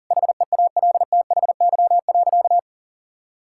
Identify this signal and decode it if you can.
Morse code, heard as a keyed tone, about 40 wpm, 700 Hz. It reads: HEALTHY.
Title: Sentences from Top 1000 Words - Review Part 5 - 40wpm